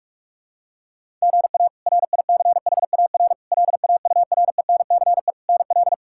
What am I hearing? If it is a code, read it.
Morse code, heard as a keyed tone, about 44 wpm, 700 Hz. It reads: GA RICHAR LAURENCE NL
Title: K1USN SST QSO Practice - 45wpm